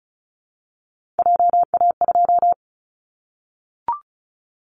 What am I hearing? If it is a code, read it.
Morse code, heard as a keyed tone, about 36 wpm, 700 Hz. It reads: JA2
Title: Callsign Prefixes for Top 50 DXCC - 35wpm